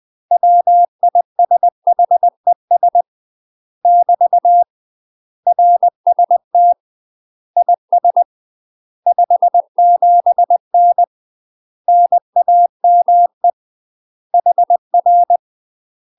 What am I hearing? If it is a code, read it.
Morse code, heard as a keyed tone, about 20 wpm, 700 Hz. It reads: WISHES <BT> RST IS 57N NAME HR
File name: UK MASTER QSOs TRAINING / 20WPM / UK MASTER QSOs 0059